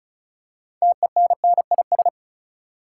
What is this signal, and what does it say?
Morse code, heard as a keyed tone, about 36 wpm, 700 Hz. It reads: TENNIS
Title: Six Letter Words - 35wpm